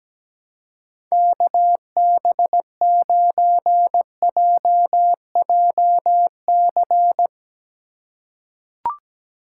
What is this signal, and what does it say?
Morse code, heard as a keyed tone, about 17 wpm, 700 Hz. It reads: KB9JJC
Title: USA Call Signs - 4x Character Spacing - 17wpm